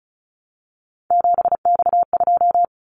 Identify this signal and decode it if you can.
Morse code, heard as a keyed tone, about 36 wpm, 700 Hz. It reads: 7X2
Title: Callsign Prefixes for Top 100 DXCC - 35wpm